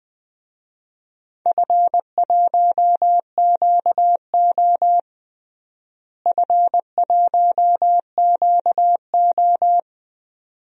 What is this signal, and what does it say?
Morse code, heard as a keyed tone, about 21 wpm, 700 Hz. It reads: F1QO F1QO
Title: DXCC-100 Call Signs - 4 Characters - Repeated 2x - 20wpm